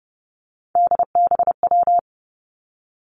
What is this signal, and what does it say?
Morse code, heard as a keyed tone, about 30 wpm, 700 Hz. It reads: DBW
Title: Three Challenging Letter-Number Combinations 30wpm